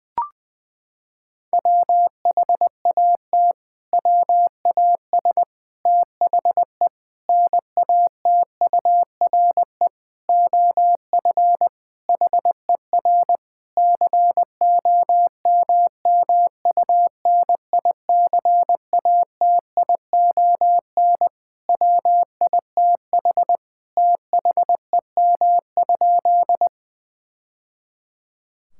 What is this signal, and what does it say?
Morse code, heard as a keyed tone, about 20 wpm, 700 Hz. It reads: WHAT WAS THE NATURE OF HER COMMUNICATION WITH THEM?
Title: Sentences from Top 1000 Words - Review Part 6 - 20wpm